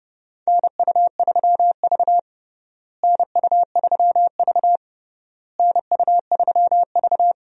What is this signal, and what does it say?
Morse code, heard as a keyed tone, about 30 wpm, 700 Hz. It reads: NU3V NU3V NU3V
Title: USA Call Signs - 4 Characters - Repeated 3x - 30wpm